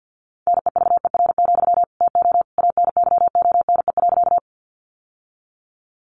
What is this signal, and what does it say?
Morse code, heard as a keyed tone, about 50 wpm, 700 Hz. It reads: NEVER, TO ANYONE.
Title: Sentences from Top 900 Words 50wpm